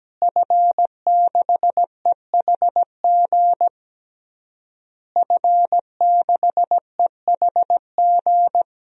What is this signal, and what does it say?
Morse code, heard as a keyed tone, about 17 wpm, 700 Hz. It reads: F6EHG F6EHG
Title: DXCC-100 Call Signs - 5 Characters - Repeated 3x - 17wpm